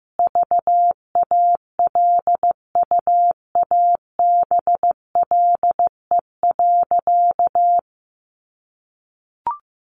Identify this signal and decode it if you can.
Morse code, heard as a keyed tone, about 15 wpm, 700 Hz. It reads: VALUABLE.
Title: Sentences from Top 1200 Words 15wpm